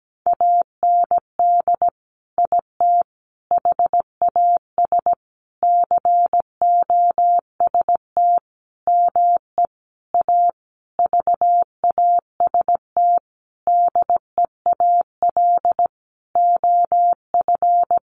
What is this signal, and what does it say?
Morse code, heard as a keyed tone, about 17 wpm, 700 Hz. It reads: AND IT HAS COST ME A VAST DEAL OF
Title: Sentences from Top 1300 Words 17wpm